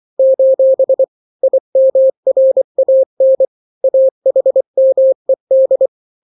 Morse code, heard as a keyed tone, about 24 wpm, 530 Hz. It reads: : IMRAN AHMED
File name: daily / 23wpm